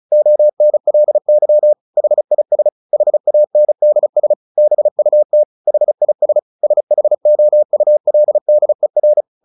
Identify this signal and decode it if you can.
Morse code, heard as a keyed tone, about 34 wpm, 600 Hz. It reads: ONLY HIS HANDS BUT HIS SHOULDER
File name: the old man and the sea 2 35wpm